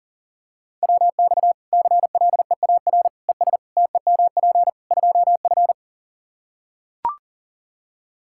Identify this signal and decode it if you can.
Morse code, heard as a keyed tone, about 40 wpm, 700 Hz. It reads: WX CLEAR ES TEMP 2F
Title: Practice QSO - Exchange 2 - Info - 40wpm